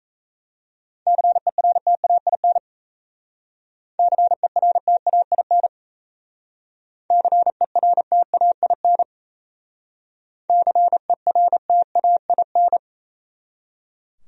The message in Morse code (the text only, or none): CERTAIN CERTAIN CERTAIN CERTAIN